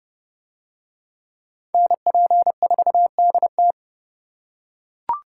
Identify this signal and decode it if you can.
Morse code, heard as a keyed tone, about 32 wpm, 700 Hz. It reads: NP4DT